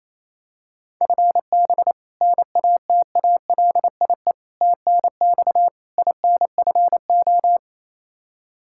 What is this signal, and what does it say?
Morse code, heard as a keyed tone, about 27 wpm, 700 Hz. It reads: FB NATALIE TNX INFO
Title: Practice QSO - Exchange 3 - More Info - 28wpm